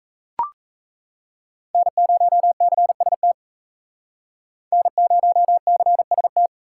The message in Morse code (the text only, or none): N0CST N0CST